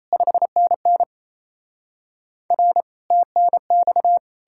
Morse code, heard as a keyed tone, about 38 wpm, 700 Hz. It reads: 5NN R TNX